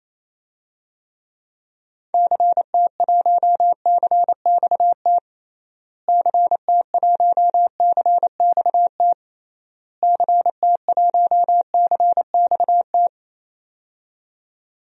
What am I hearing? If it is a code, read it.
Morse code, heard as a keyed tone, about 28 wpm, 700 Hz. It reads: CT1CXT CT1CXT CT1CXT
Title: DXCC-100 Call Signs - 6 Characters - Repeated 3x - 28wpm